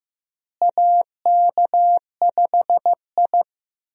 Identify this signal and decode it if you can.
Morse code, heard as a keyed tone, about 16 wpm, 700 Hz. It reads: AK5I
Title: USA Call Signs - 4 Characters - Repeated 3x - 15wpm